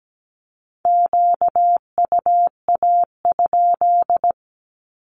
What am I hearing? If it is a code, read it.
Morse code, heard as a keyed tone, about 17 wpm, 700 Hz. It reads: QUA?
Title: Q-Codes 17wpm